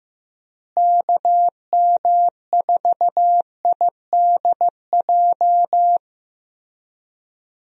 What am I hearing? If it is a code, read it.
Morse code, heard as a keyed tone, about 15 wpm, 700 Hz. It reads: KM4IDJ
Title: USA Call Signs - 6 Characters - Repeated 3x - 15wpm